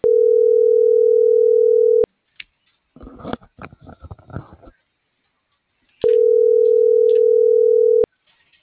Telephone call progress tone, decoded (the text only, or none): ringback tone